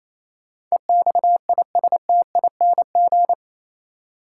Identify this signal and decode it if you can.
Morse code, heard as a keyed tone, about 27 wpm, 700 Hz. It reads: EXISTING
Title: Sentences from Top 1500 Words - New Word Drill Alt - 28wpm